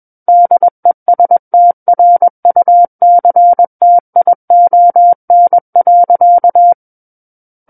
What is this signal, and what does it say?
Morse code, heard as a keyed tone, about 21 wpm, 700 Hz. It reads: DESTRUCTION.
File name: Book One / 12-Chapter 12 - 21 WPM